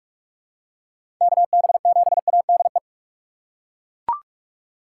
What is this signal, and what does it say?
Morse code, heard as a keyed tone, about 41 wpm, 700 Hz. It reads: KB7ADE
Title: USA Call Signs - 6 Characters - 4x Character Spacing - 45wpm